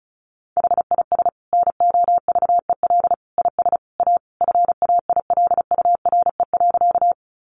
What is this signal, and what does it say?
Morse code, heard as a keyed tone, about 34 wpm, 700 Hz. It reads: HIS NOVEL IS A FAILURE.
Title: Sentences from Top 1300 Words 35wpm